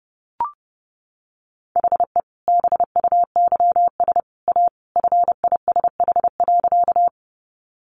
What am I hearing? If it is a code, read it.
Morse code, heard as a keyed tone, about 30 wpm, 700 Hz. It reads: HE BUYS A FISH.
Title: Additional Easy Phrases 30wpm